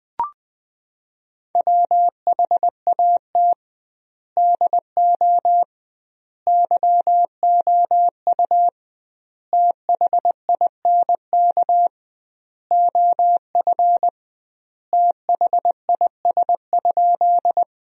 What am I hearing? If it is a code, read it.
Morse code, heard as a keyed tone, about 21 wpm, 700 Hz. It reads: WHAT DO YOU THINK OF THIS?
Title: 2x Word-Spacing - Sentences from Top 200 Words - 20wpm